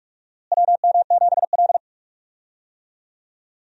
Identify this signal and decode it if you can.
Morse code, heard as a keyed tone, about 42 wpm, 700 Hz. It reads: WM7L
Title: USA Call Signs - 4 Characters - Repeated 3x - 45wpm